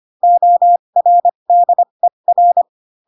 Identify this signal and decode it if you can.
Morse code, heard as a keyed tone, about 25 wpm, spaced as for 22 wpm, 700 Hz. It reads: ORDER
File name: Pinocchio / Pinocchio ch8 18 25 ik1hge